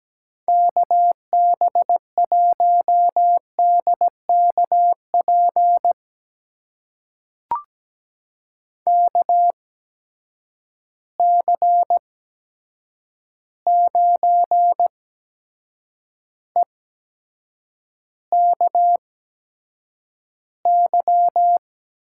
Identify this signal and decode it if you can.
Morse code, heard as a keyed tone, about 17 wpm, spaced as for 7 wpm, 700 Hz. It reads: KB1DKP KC9EKY
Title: USA Call Signs - 6 Characters - 8x Character Spacing - 17wpm